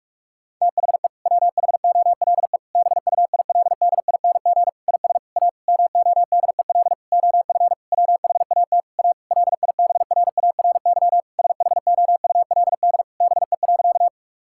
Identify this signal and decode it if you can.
Morse code, heard as a keyed tone, about 46 wpm, 700 Hz. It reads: THE WHOLE BUILDING IS A MODEL OF WHAT A LIBRARY SHOULD BE.
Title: Sentences from Top 1000 Words - Review Part 5 - 45wpm